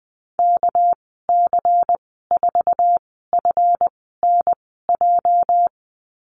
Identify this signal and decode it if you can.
Morse code, heard as a keyed tone, about 20 wpm, spaced as for 15 wpm, 700 Hz. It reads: KC4FNJ
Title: 231 Lesson 37g - Letter x - Call Signs 20wpm